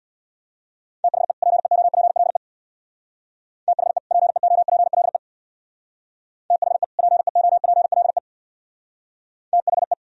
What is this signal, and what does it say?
Morse code, heard as a keyed tone, about 63 wpm, 700 Hz. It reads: THE PEOPLE THE PEOPLE THE PEOPLE THE